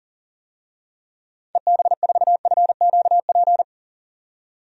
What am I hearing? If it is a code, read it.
Morse code, heard as a keyed tone, about 37 wpm, 700 Hz. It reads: EB4FQP
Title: DXCC-100 Call Signs - 6 Characters - 40wpm